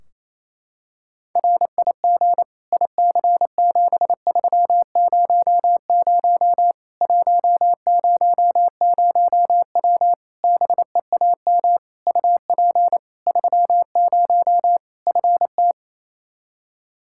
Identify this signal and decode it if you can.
Morse code, heard as a keyed tone, about 28 wpm, 700 Hz. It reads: RIG IC7300 100W BEAM UP 30 FT
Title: Practice QSO - Exchange 2 - Info - 28wpm